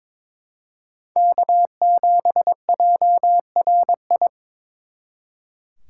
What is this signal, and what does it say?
Morse code, heard as a keyed tone, about 26 wpm, 700 Hz. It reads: K7JRI
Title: Speed Racing - USA 1x3 Call Signs 22wpm